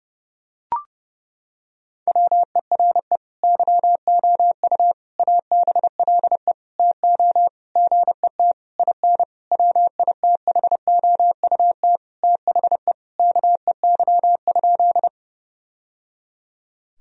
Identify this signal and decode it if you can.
Morse code, heard as a keyed tone, about 30 wpm, 700 Hz. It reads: WERE YOU ABLE TO GET IN WITHOUT THE KEY?
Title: Sentences from Top 1000 Words - Review Part 4 - 30wpm